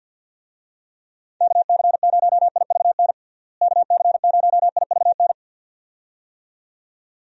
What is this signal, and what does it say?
Morse code, heard as a keyed tone, about 50 wpm, 700 Hz. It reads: KC0IVN KC0IVN
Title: USA Call Signs - 6 Characters - Repeated 2x - 50wpm